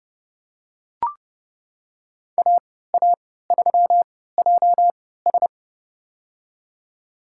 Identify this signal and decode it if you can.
Morse code, heard as a keyed tone, about 30 wpm, spaced as for 17 wpm, 700 Hz. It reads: AA3JS
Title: USA Call Signs - 5 Characters - 3x Character Spacing - 30wpm